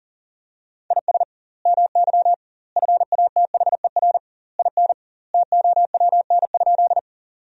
Morse code, heard as a keyed tone, about 40 wpm, 700 Hz. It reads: IS MY FATHER IN TOWN?
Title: 2x Word-Spacing - Sentences from Top 500 Words - 40wpm